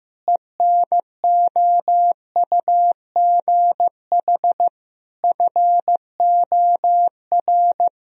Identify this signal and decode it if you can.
Morse code, heard as a keyed tone, about 15 wpm, 700 Hz. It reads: ENOUGH FOR
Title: Sentences from Top 1000 Words - Review Part 2 - 15wpm